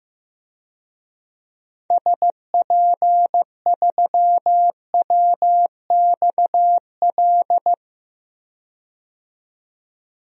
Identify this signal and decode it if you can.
Morse code, heard as a keyed tone, about 16 wpm, 700 Hz. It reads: SP3WXL
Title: DXCC-100 Call Signs - 6 Characters - 15wpm